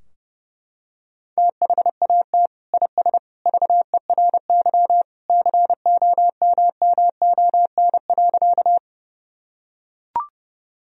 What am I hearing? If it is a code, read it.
Morse code, heard as a keyed tone, about 30 wpm, 700 Hz. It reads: THAT IS VERY COMMON.
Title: Sentences from Top 1000 Words - Review Part 3 - 30wpm